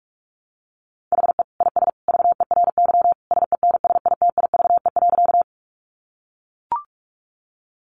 Morse code, heard as a keyed tone, about 44 wpm, 700 Hz. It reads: HE IS VERY SENSITIVE.
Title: Sentences from Top 1400 Words 45wpm